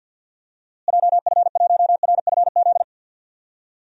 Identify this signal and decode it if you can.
Morse code, heard as a keyed tone, about 49 wpm, 700 Hz. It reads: JF1RFZ
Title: DXCC-100 Call Signs - 6 Characters - Repeated 2x - 50wpm